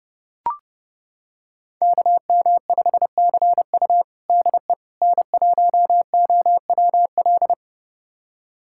Morse code, heard as a keyed tone, about 30 wpm, 700 Hz. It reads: KM5CU DE N1OWL
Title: USA Callsign DE Callsign - 5 Characters - 30wpm